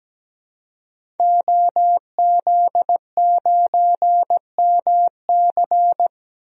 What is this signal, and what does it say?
Morse code, heard as a keyed tone, about 17 wpm, 700 Hz. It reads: OZ9MC
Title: DXCC-100 Call Signs - 5 Characters - 17wpm